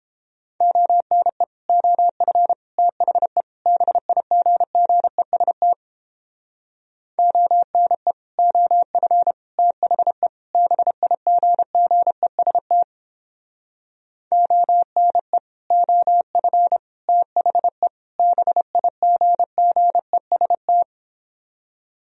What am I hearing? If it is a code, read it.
Morse code, heard as a keyed tone, about 31 wpm, 700 Hz. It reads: ONE OF THE BIGGEST ONE OF THE BIGGEST ONE OF THE BIGGEST